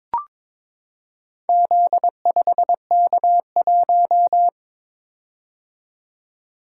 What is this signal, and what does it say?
Morse code, heard as a keyed tone, about 22 wpm, 700 Hz. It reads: Z5K1
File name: practice / 22wpm / Four Letter-Number Combinations - 22wpm